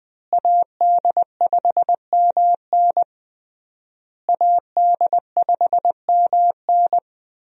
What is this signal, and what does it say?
Morse code, heard as a keyed tone, about 21 wpm, 700 Hz. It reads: AD5MN AD5MN
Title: USA Call Signs - 5 Characters - Repeated 3x - 20wpm